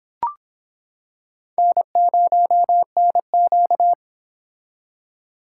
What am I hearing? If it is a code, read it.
Morse code, heard as a keyed tone, about 26 wpm, 700 Hz. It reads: N0NQ